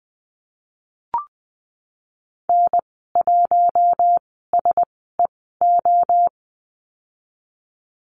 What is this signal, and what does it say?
Morse code, heard as a keyed tone, about 20 wpm, spaced as for 15 wpm, 700 Hz. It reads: N1SEO
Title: 048 Lesson 9g - Number 4 - Call Signs 20wpm